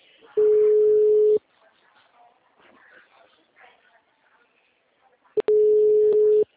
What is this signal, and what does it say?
Ringback tone (Europe, 425 Hz), with speech.